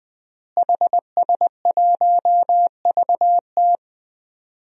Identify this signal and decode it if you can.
Morse code, heard as a keyed tone, about 21 wpm, 700 Hz. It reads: HS1VT
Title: DXCC-100 Call Signs - 5 Characters - Repeated 3x - 20wpm